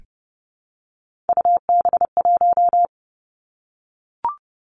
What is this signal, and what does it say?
Morse code, heard as a keyed tone, about 30 wpm, 700 Hz. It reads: UB1
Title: Three Challenging Letter-Number Combinations 30wpm